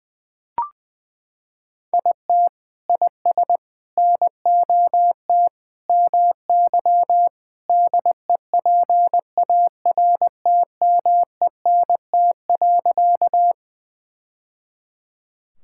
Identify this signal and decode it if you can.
Morse code, heard as a keyed tone, about 20 wpm, 700 Hz. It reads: IT IS NOT MY DEPARTMENT.
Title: Sentences from Top 1000 Words - Review Part 5 - 20wpm